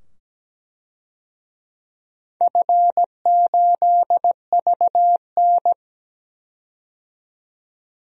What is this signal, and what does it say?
Morse code, heard as a keyed tone, about 17 wpm, 700 Hz. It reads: F8VN
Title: DXCC-100 Call Signs - 4 Characters - 17wpm